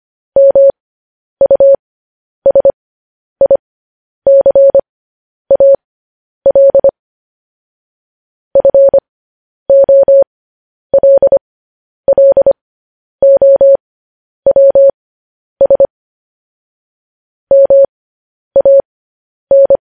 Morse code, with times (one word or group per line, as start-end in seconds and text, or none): 0.4-6.9 MUSICAL
8.6-15.9 FOLLOWS
17.5-19.8 MAN